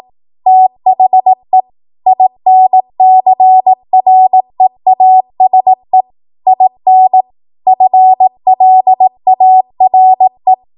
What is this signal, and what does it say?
Morse code, heard as a keyed tone, about 18 wpm, 750 Hz. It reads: THE INCREASE IN FLARE
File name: Morse / 18WPM